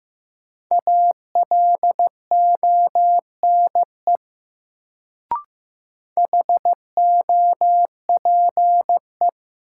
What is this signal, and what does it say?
Morse code, heard as a keyed tone, about 15 wpm, 700 Hz. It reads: ALONE HOPE
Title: Sentences from Top 1000 Words - New Word Drill - 15wpm